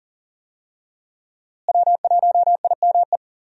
Morse code, heard as a keyed tone, about 39 wpm, 700 Hz. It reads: W1IME
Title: USA Call Signs - 5 Characters - Repeated 2x - 40wpm